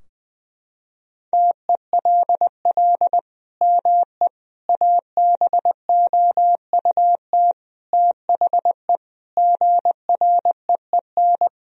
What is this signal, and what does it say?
Morse code, heard as a keyed tone, about 20 wpm, 700 Hz. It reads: TELL ME ABOUT THE GREEN
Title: Sentences from Top 1000 Words - Review Part 2 - 20wpm